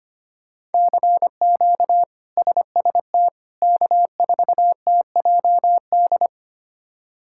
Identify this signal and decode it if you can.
Morse code, heard as a keyed tone, about 25 wpm, 700 Hz. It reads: CQ SST K4TJD
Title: K1USN SST QSO Practice - 25wpm